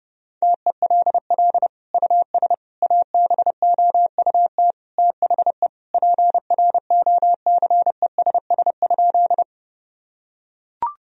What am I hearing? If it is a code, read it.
Morse code, heard as a keyed tone, about 30 wpm, 700 Hz. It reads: TELL US ABOUT THE PROCESS?
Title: Sentences from Top 1000 Words - Review Part 3 - 30wpm